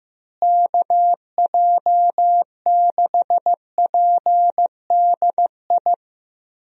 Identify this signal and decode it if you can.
Morse code, heard as a keyed tone, about 15 wpm, 700 Hz. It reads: KJ6PDI
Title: USA Call Signs - 6 Characters - Repeated 3x - 15wpm